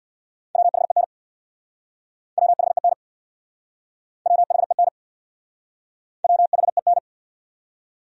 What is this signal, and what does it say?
Morse code, heard as a keyed tone, about 66 wpm, 700 Hz. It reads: WHEN WHEN WHEN WHEN